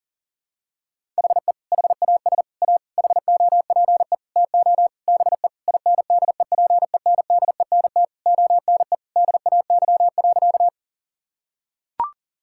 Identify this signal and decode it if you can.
Morse code, heard as a keyed tone, about 37 wpm, 700 Hz. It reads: HE HAS A HOPE TO BE INDEPENDENT ONE DAY.
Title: Sentences from Top 1000 Words - Review Part 6 - 40wpm